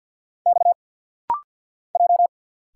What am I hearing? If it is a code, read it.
Morse code, heard as a keyed tone, about 50 wpm, 700 Hz. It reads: X J
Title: Single Letters - Rapid-Fire 50wpm